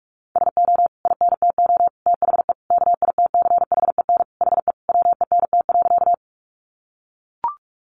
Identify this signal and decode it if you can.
Morse code, heard as a keyed tone, about 44 wpm, 700 Hz. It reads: SO INTO THE KITCHEN HE WENT.
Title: Sentences from Top 1400 Words 45wpm